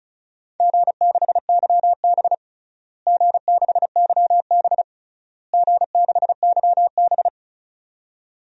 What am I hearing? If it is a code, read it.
Morse code, heard as a keyed tone, about 34 wpm, 700 Hz. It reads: G6YB G6YB G6YB
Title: DXCC-100 Call Signs - 4 Characters - Repeated 3x - 35wpm